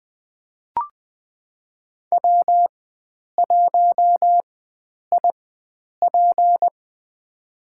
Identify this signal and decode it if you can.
Morse code, heard as a keyed tone, about 20 wpm, spaced as for 9 wpm, 700 Hz. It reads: W1IP
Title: USA Call Signs - 4 Characters - 4x Character Spacing - 20wpm